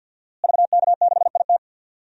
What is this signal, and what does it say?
Morse code, heard as a keyed tone, about 52 wpm, 700 Hz. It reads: VK6IT
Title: DXCC-100 Call Signs - 5 Characters - Repeated 3x - 50wpm